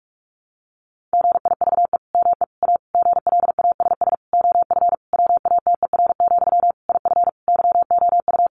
Morse code, heard as a keyed tone, about 44 wpm, 700 Hz. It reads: GIVE ME A GLASS OF WATER, IF YOU
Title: Sentences from Top 1500 Words 45wpm